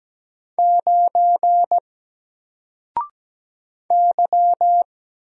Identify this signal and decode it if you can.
Morse code, heard as a keyed tone, about 17 wpm, 700 Hz. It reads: 9 Y
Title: Single Character (Letter:Number:Punctuation:Prosign) 17wpm